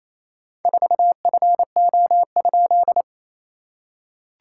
Morse code, heard as a keyed tone, about 27 wpm, 700 Hz. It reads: 4FO?
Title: SOTA QSO Practice 28wpm